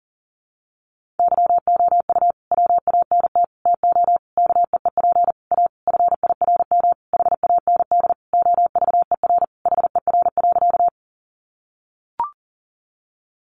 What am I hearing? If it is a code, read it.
Morse code, heard as a keyed tone, about 37 wpm, 700 Hz. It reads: YOU WANT TO KEEP A FIRM HAND OVER HER.
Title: Sentences from Top 1400 Words 40wpm